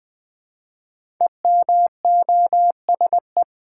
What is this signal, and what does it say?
Morse code, heard as a keyed tone, about 20 wpm, 700 Hz. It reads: EMOSE
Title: Code Groups - 5 Letters - 20wpm